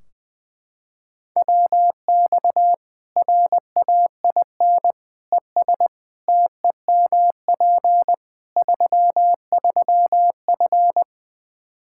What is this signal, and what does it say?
Morse code, heard as a keyed tone, about 20 wpm, 700 Hz. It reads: WX RAIN ES TEMP 33F
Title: Practice QSO - Exchange 2 - Info - 20wpm